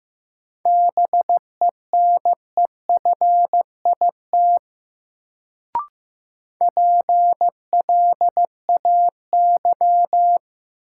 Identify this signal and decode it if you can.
Morse code, heard as a keyed tone, about 15 wpm, 700 Hz. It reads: BENEFIT PLAY
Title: Sets of 5 Words Encore - New Word Drill - 15wpm